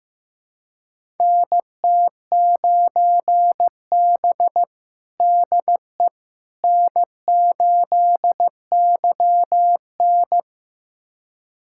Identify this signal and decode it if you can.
Morse code, heard as a keyed tone, about 15 wpm, 700 Hz. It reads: NT9B DE N8YN
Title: USA Callsign DE Callsign - 4 Characters - 15wpm